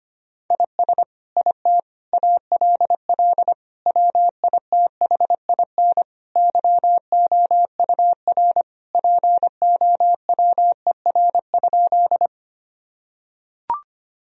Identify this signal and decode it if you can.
Morse code, heard as a keyed tone, about 25 wpm, 700 Hz. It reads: IS IT ALL WITHIN YOUR POWER?
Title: Sentences from Top 1000 Words - Review Part 2 - 25wpm